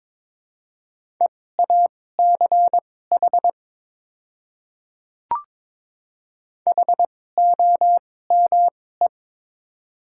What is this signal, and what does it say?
Morse code, heard as a keyed tone, about 21 wpm, spaced as for 16 wpm, 700 Hz. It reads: EACH HOME
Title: 2x Character Spacing - Top 500 Words - 22wpm